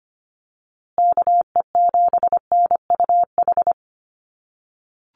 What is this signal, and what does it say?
Morse code, heard as a keyed tone, about 25 wpm, 700 Hz. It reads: KE7NUH
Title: USA Call Signs Repeated 3x 25wpm